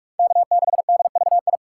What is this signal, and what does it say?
Morse code, heard as a keyed tone, about 47 wpm, 700 Hz. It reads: K6DVI
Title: USA Call Signs - 6x Character Spacing - 45wpm